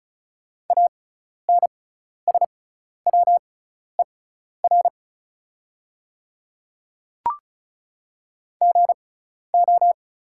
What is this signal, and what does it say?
Morse code, heard as a keyed tone, about 34 wpm, spaced as for 12 wpm, 700 Hz. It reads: ANSWER GO
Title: 6x Character Spacing - Top 300 Words - 35wpm